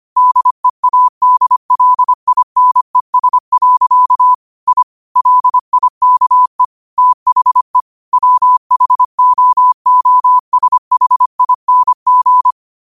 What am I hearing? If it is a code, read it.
Morse code, heard as a keyed tone, about 25 wpm, 1000 Hz. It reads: DEADLINES. I LIKE THE WHOOSHING
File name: qotdfiles / 25wpm